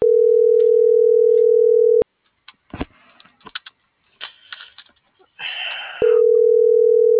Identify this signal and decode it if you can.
Ringback tone (North America), 440 + 480 Hz.